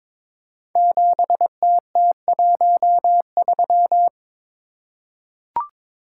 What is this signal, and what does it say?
Morse code, heard as a keyed tone, about 22 wpm, 690 Hz. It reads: 7TT13
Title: Code Groups - 5 Letter:Numbers - 22wpm